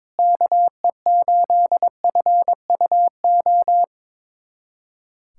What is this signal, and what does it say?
Morse code, heard as a keyed tone, about 22 wpm, 700 Hz. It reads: KE8FUO
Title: USA Callsign DE Callsign - 6 Characters - 22wpm